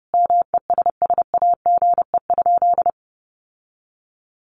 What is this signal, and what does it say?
Morse code, heard as a keyed tone, about 30 wpm, 700 Hz. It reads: MESSAGE?
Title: Sentences from Top 1100 Words 30wpm